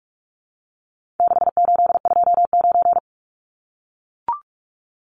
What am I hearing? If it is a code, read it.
Morse code, heard as a keyed tone, about 44 wpm, 700 Hz. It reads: 6829
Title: Numbers - 4-Digit 45wpm